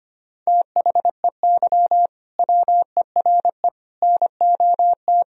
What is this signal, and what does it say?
Morse code, heard as a keyed tone, about 25 wpm, 700 Hz. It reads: THEY WERE NOT
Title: Sentences from Top 1000 Words - Review Part 3 - 25wpm